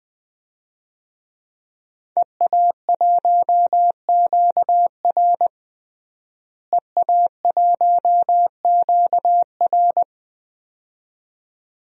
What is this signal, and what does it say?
Morse code, heard as a keyed tone, about 20 wpm, 700 Hz. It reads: EA1QR EA1QR
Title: DXCC-100 Call Signs - 5 Characters - Repeated 2x - 20wpm